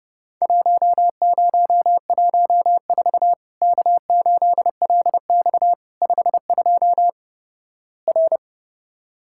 Morse code, heard as a keyed tone, about 31 wpm, 700 Hz. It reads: 1014 K8LX 52 R
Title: Worked All Europe DX Contest - QTC Exchange - 30wpm